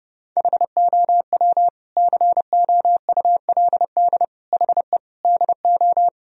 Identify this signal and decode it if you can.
Morse code, heard as a keyed tone, about 30 wpm, 700 Hz. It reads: HOW COULD HE DO
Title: Sentences from Top 1000 Words - Review Part 1 - 30wpm